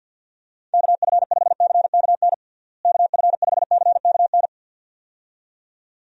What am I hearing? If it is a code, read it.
Morse code, heard as a keyed tone, about 51 wpm, 700 Hz. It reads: KF5CKN KF5CKN
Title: USA Call Signs - 6 Characters - Repeated 2x - 50wpm